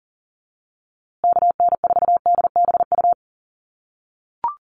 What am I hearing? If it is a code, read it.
Morse code, heard as a keyed tone, about 40 wpm, 700 Hz. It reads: KN4DBU
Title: USA Call Signs Repeated 3x 40wpm